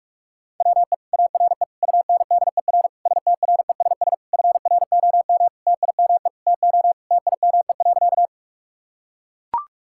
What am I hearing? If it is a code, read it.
Morse code, heard as a keyed tone, about 41 wpm, 700 Hz. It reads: WE ARE UNDER STRESS FROM TIME TO TIME.
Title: Sentences from Top 1000 Words - Review Part 6 - 45wpm